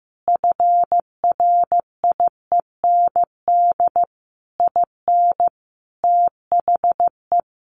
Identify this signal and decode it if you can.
Morse code, heard as a keyed tone, about 15 wpm, 700 Hz. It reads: FRIEND IN THE
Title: Sentences from Top 1100 Words 15wpm